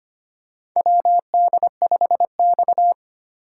Morse code, heard as a keyed tone, about 25 wpm, 700 Hz. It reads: WD5X
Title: USA Call Signs - 4 Characters - 8x Character Spacing - 25wpm